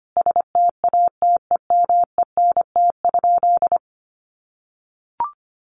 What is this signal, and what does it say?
Morse code, heard as a keyed tone, about 25 wpm, 700 Hz. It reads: STATEMENT?
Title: Sentences from Top 1200 Words 25wpm